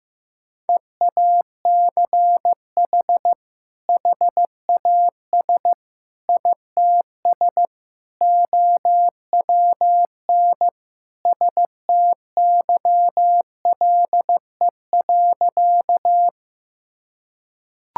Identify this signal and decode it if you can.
Morse code, heard as a keyed tone, about 16 wpm, 700 Hz. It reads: EACH HAS ITS OWN STYLE.
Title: Sentences from Top 1000 Words - Review Part 5 - 15wpm